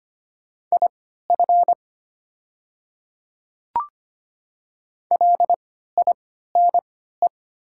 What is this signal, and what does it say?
Morse code, heard as a keyed tone, about 25 wpm, spaced as for 14 wpm, 700 Hz. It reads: IF LINE
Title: 3x Character Spacing - Top 300 Words - 25wpm